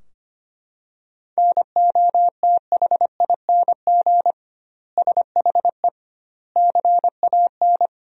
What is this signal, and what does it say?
Morse code, heard as a keyed tone, about 25 wpm, 700 Hz. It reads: NOTHING SHE CAN
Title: 2x Word-Spacing - Sentences from Top 500 Words - 25wpm